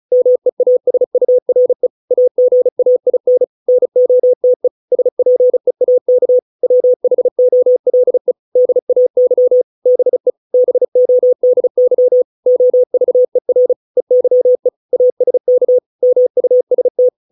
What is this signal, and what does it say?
Morse code, heard as a keyed tone, about 34 wpm, 500 Hz. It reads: MEASURE AGAIN NOTE SPEAK WHOLE DAY BE BODY OVER EYE ASK MUST